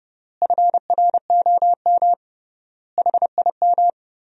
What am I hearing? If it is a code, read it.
Morse code, heard as a keyed tone, about 32 wpm, 700 Hz. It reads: FROM HIM
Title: 3x Word-Spacing - Sentences from Top 400 Words - 30wpm